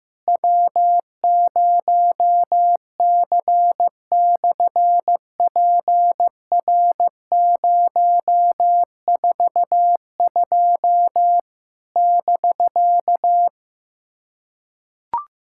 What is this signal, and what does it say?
Morse code, heard as a keyed tone, about 15 wpm, 700 Hz. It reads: W0C/PR042 <BK>
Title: SOTA QSO Practice 15wpm